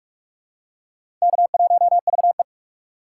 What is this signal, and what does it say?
Morse code, heard as a keyed tone, about 44 wpm, 700 Hz. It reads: K1VE
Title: USA Call Signs - 4 Characters - 8x Character Spacing - 45wpm